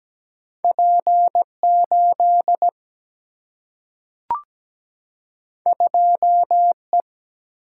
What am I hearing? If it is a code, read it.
Morse code, heard as a keyed tone, about 17 wpm, 700 Hz. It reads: P8 2E
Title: Single-Letter Single-Number 17wpm